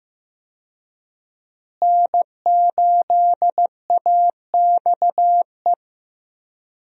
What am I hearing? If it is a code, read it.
Morse code, heard as a keyed tone, about 15 wpm, 700 Hz. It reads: N8AXE
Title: USA Call Signs - 5 Characters - Repeated 3x - 15wpm